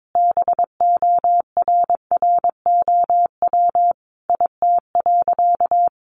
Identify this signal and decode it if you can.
Morse code, heard as a keyed tone, about 22 wpm, 700 Hz. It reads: BORROW IT.
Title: Sentences from Top 1500 Words 22wpm